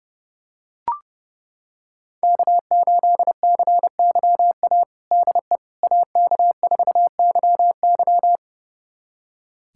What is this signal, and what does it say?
Morse code, heard as a keyed tone, about 32 wpm, 700 Hz. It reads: K8CYA DE AK4YY
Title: USA Callsign DE Callsign - 5 Characters - 30wpm